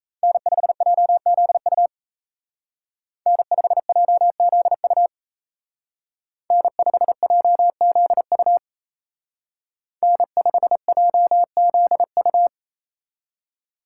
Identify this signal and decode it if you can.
Morse code, heard as a keyed tone, about 36 wpm, 700 Hz. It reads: N5JZU N5JZU N5JZU N5JZU